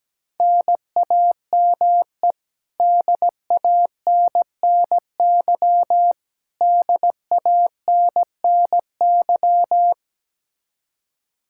Morse code, heard as a keyed tone, about 17 wpm, 700 Hz. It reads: NAME DANNY DANNY
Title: Practice QSO - Exchange 1 - Beginning - 17wpm